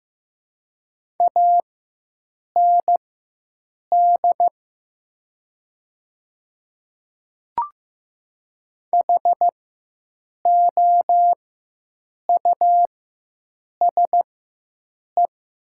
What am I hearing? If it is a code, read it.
Morse code, heard as a keyed tone, about 15 wpm, spaced as for 7 wpm, 700 Hz. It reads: AND HOUSE
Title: 4x Character Spacing - Top 300 Words - 15wpm